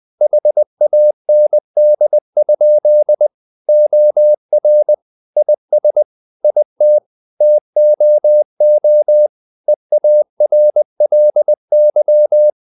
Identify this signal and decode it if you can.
Morse code, heard as a keyed tone, about 20 wpm, 600 Hz. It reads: HAND? OR IS IT TOO EARLY
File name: the old man and the sea / the old man and the sea part 1 20wpm